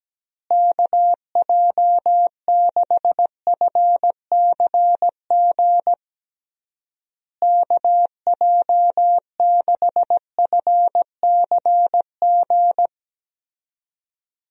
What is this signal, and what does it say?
Morse code, heard as a keyed tone, about 17 wpm, 700 Hz. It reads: KJ6FCG KJ6FCG
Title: USA Call Signs - 6 Characters - Repeated 3x - 17wpm